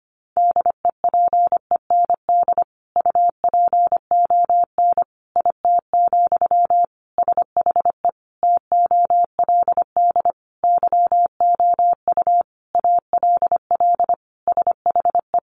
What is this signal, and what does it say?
Morse code, heard as a keyed tone, about 25 wpm, 700 Hz. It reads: DEPEND UPON IT, SHE TOLD YOU ALL SHE
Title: Sentences from Top 1300 Words 25wpm